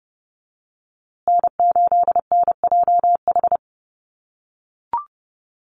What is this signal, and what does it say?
Morse code, heard as a keyed tone, about 30 wpm, 700 Hz. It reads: N8NJH
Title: USA Call Signs Repeated 3x 30wpm